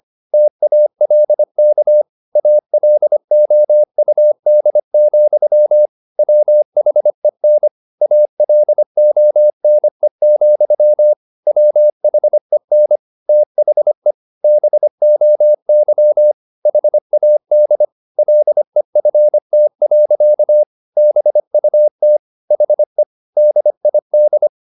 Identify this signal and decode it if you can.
Morse code, heard as a keyed tone, about 25 wpm, 600 Hz. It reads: TALK ALOUD, WHEN ALONE, WHEN THE BOY HAD LEFT. BUT HE DID
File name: the old man and the sea part 1 25wpm